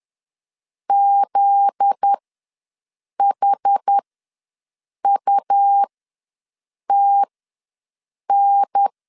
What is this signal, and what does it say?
Morse code, heard as a keyed tone, about 10 wpm, spaced as for 6 wpm, 800 Hz. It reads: ZHUTN
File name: LEZIONI / 8 YJ45 / 8 YJ4 ws 10wpm 4020